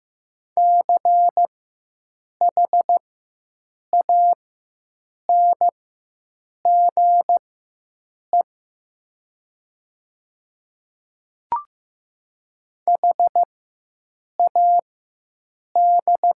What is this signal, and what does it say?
Morse code, heard as a keyed tone, about 15 wpm, spaced as for 7 wpm, 700 Hz. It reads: CHANGE HAD
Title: 4x Character Spacing - Top 300 Words - 15wpm